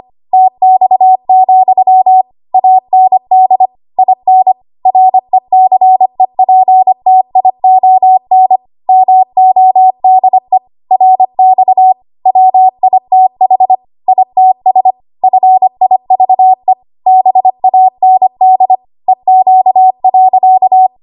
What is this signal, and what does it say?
Morse code, heard as a keyed tone, about 25 wpm, 750 Hz. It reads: TX, AND IN RECEPTION MODE RX WITH ITS FIVE BAND EQ.
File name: Archive / 25 WPM / 25WPM